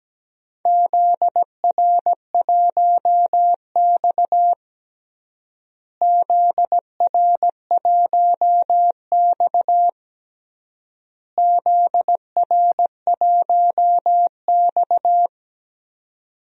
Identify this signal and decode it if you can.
Morse code, heard as a keyed tone, about 17 wpm, 700 Hz. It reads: ZR1X ZR1X ZR1X
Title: DXCC-100 Call Signs - 4 Characters - Repeated 3x - 17wpm